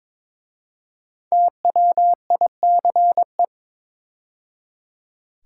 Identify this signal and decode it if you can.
Morse code, heard as a keyed tone, about 23 wpm, 700 Hz. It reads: TWICE
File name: 5 Letter Words - First 3 Letters Given - 22wpm